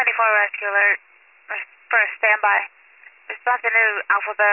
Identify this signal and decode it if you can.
Emergency Alert System (SAME) tones, heard as harsh buzzing bursts, with speech. It is heard from 0.3 to 3.4 s.